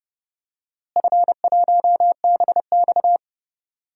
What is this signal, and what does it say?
Morse code, heard as a keyed tone, about 30 wpm, 700 Hz. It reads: F1BX
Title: DXCC-100 Call Signs - 4 Characters - Repeated 2x - 30wpm